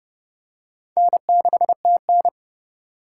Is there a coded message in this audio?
Yes: Morse code from 1.0 to 2.3 s, heard as a keyed tone, about 31 wpm, 700 Hz. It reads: N6TN